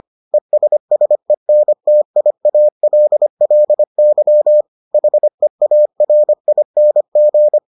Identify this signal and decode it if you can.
Morse code, heard as a keyed tone, about 25 wpm, 600 Hz. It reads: ESSENTIALLY HEARING